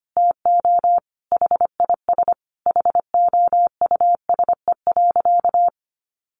Morse code, heard as a keyed tone, about 25 wpm, 700 Hz. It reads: TO HIS HOUSE.
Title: Easy Phrases 25wpm